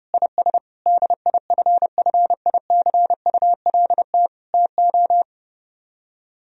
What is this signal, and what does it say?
Morse code, heard as a keyed tone, about 30 wpm, 700 Hz. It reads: IS DIFFICULT TO